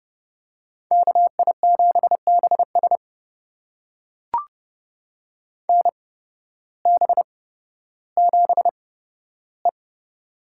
Morse code, heard as a keyed tone, about 30 wpm, spaced as for 14 wpm, 700 Hz. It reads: KI7BS NB7E
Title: USA Call Signs - 8x Character Spacing - 30wpm